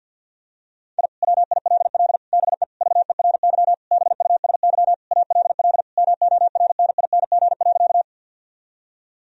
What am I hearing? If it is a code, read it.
Morse code, heard as a keyed tone, about 50 wpm, 700 Hz. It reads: I WILL BE VERY BUSY ALL MORNING.
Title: Sentences from Top 1000 Words - Review Part 3 - 50wpm